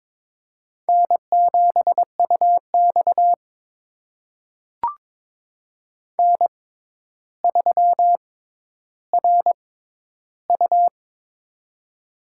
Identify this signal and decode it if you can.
Morse code, heard as a keyed tone, about 22 wpm, spaced as for 11 wpm, 700 Hz. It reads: N7UX N3RU
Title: USA Call Signs - 4 Characters - 6x Character Spacing - 22wpm